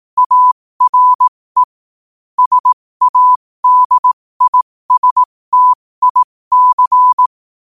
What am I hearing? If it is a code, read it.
Morse code, heard as a keyed tone, about 18 wpm, spaced as for 15 wpm, 1000 Hz. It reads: ARE SADISTIC
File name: qotdfiles / 15wpm